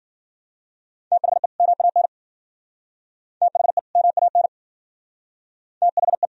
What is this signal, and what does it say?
Morse code, heard as a keyed tone, about 50 wpm, 700 Hz. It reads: THE MAN THE MAN THE